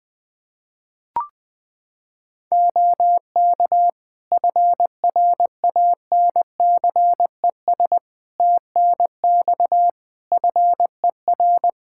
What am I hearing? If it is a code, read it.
Morse code, heard as a keyed tone, about 21 wpm, 700 Hz. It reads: OK FRANCES TNX FER
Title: Practice QSO - Exchange 4 - Ending - 20wpm